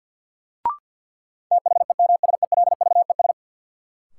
Morse code, heard as a keyed tone, about 46 wpm, 700 Hz. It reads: THEMSELVES
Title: Sentences from Top 800 Words - New Word Drill - 50wpm